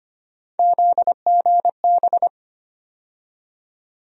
Morse code, heard as a keyed tone, about 25 wpm, 700 Hz. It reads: ZGB
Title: USA Call Sign Postfixes - 3 Letters 25wpm